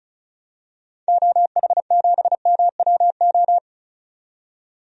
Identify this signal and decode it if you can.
Morse code, heard as a keyed tone, about 36 wpm, 700 Hz. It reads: OH7MWO
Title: DXCC-100 Call Signs - 6 Characters - 35wpm